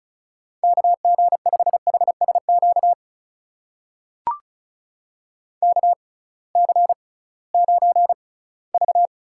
Morse code, heard as a keyed tone, about 36 wpm, spaced as for 20 wpm, 700 Hz. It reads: KG5HSQ KC9V
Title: USA Call Signs - 6 Characters - 6x Character Spacing - 35wpm